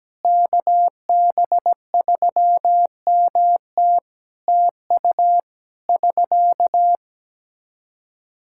Morse code, heard as a keyed tone, about 17 wpm, 700 Hz. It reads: KB3MT TU <SK>
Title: K1USN SST QSO Practice 17wpm